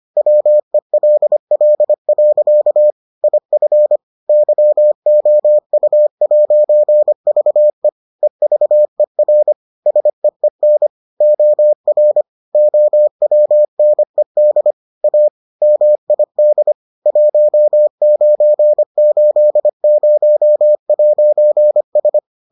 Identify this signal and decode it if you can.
Morse code, heard as a keyed tone, about 25 wpm, 600 Hz. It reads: WELL. IF YOU'VE EVER SEEN OR OWNED A MID 1980'S